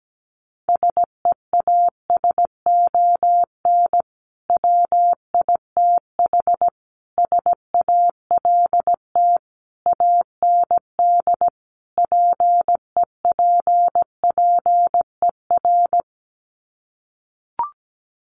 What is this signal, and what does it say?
Morse code, heard as a keyed tone, about 17 wpm, 700 Hz. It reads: SEASON WITH SALT AND PEPPER
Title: Sets of 5 Words 17wpm